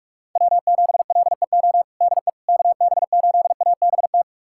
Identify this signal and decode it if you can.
Morse code, heard as a keyed tone, about 42 wpm, 700 Hz. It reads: W7LEO DE KB8ABT
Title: USA Callsign DE Callsign 45wpm